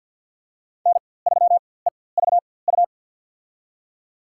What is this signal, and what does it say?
Morse code, heard as a keyed tone, about 53 wpm, spaced as for 24 wpm, 700 Hz. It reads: N3EVU
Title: USA Call Signs - 4x Character Spacing - 50wpm